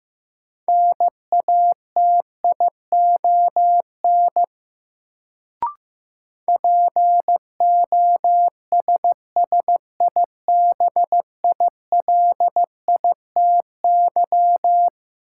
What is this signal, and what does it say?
Morse code, heard as a keyed tone, about 15 wpm, 700 Hz. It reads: NATION POSSIBILITY
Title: Sets of 3 Words - New Word Drill - 15wpm